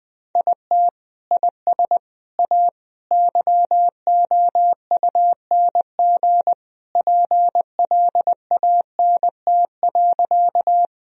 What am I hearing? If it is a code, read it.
Morse code, heard as a keyed tone, about 20 wpm, 700 Hz. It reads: IT IS A YOUNG PLANT.
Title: Sentences from Top 1000 Words - Review Part 1 - 20wpm